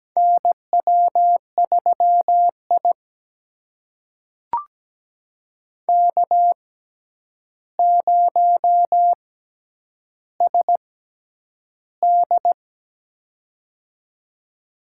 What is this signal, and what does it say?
Morse code, heard as a keyed tone, about 17 wpm, spaced as for 9 wpm, 700 Hz. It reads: NW3I K0SD
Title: USA Call Signs - 4 Characters - 6x Character Spacing - 17wpm